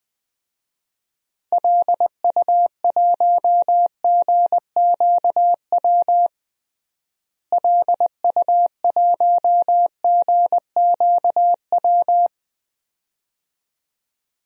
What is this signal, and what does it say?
Morse code, heard as a keyed tone, about 21 wpm, 690 Hz. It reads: LU1GQW LU1GQW
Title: DXCC-100 Call Signs - 6 Characters - Repeated 2x - 20wpm